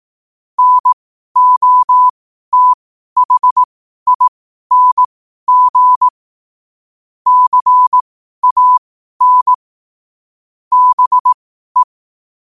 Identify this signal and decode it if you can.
Morse code, heard as a keyed tone, about 18 wpm, spaced as for 12 wpm, 1000 Hz. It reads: NOTHING CAN BE
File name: qotdfiles / 12wpm